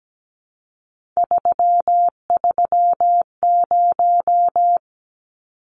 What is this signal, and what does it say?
Morse code, heard as a keyed tone, about 17 wpm, 700 Hz. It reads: 330
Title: Numbers - 3-Digit 17wpm